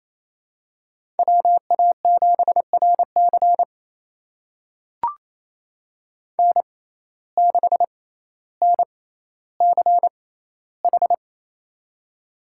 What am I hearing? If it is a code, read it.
Morse code, heard as a keyed tone, about 28 wpm, spaced as for 14 wpm, 700 Hz. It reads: WA7RC N6NCH
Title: USA Call Signs - 5 Characters - 6x Character Spacing - 28wpm